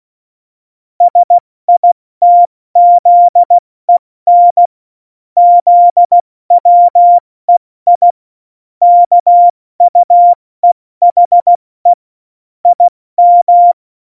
Morse code, heard as a keyed tone, about 16 wpm, 700 Hz. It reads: SITZEN ZWEI KUEHE IM
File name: Morsekurs / Witze / Witz 19. Juni 2018, 70 F80